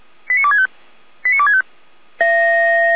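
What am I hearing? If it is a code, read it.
Five-tone selective call (ZVEI), heard as quick runs of five tones, tones of about 70 ms. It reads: 89265 89265
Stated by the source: ZVEI five-tone call received by SDR, fire brigade feuerwehr-orlamuende.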